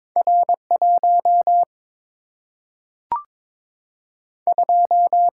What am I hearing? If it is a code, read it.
Morse code, heard as a keyed tone, about 22 wpm, 700 Hz. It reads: R 1 2
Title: Single-Letter Single-Number 22wpm